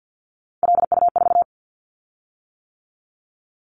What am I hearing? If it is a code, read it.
Morse code, heard as a keyed tone, about 50 wpm, 700 Hz. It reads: LU4
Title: Callsign Prefixes for Top 50 DXCC - 50wpm